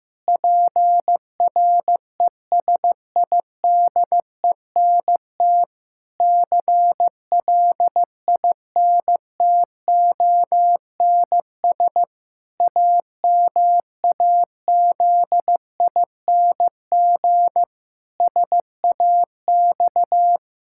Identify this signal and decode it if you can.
Morse code, heard as a keyed tone, about 15 wpm, 700 Hz. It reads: PRESIDENT CLINTONS AMAZING SAX
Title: Pangram Challenge v2 - 15wpm